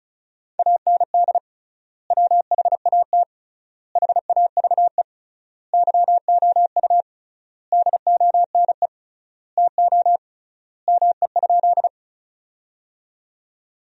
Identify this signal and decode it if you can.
Morse code, heard as a keyed tone, about 34 wpm, 700 Hz. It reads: AND WHAT HAVE YOU DONE TO ME?
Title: 3x Word-Spacing - Sentences from Top 500 Words - 35wpm